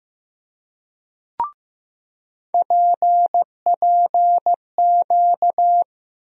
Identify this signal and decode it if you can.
Morse code, heard as a keyed tone, about 15 wpm, 700 Hz. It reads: PPQ
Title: USA Call Sign Postfixes - 3 Letters 15wpm